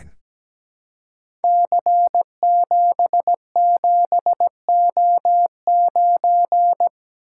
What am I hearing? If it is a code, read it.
Morse code, heard as a keyed tone, about 17 wpm, 700 Hz. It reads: C77O9
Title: Code Groups - 5 Letter:Numbers - 17wpm